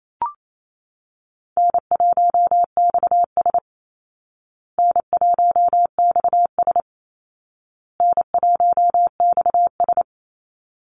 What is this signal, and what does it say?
Morse code, heard as a keyed tone, about 28 wpm, 700 Hz. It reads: N1XS N1XS N1XS
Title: Easy Call Signs Repeated 3x - 28wpm